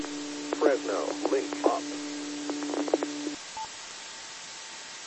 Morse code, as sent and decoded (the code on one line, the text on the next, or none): none